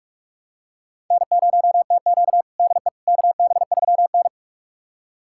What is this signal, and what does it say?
Morse code, heard as a keyed tone, about 44 wpm, 690 Hz. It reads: N0TQ DE KB3N
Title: USA Callsign DE Callsign - 4 Characters - 45wpm